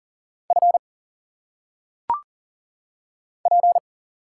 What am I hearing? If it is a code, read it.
Morse code, heard as a keyed tone, about 40 wpm, 700 Hz. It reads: F P
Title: Single Letters - Phonetic Pronunciation 40wpm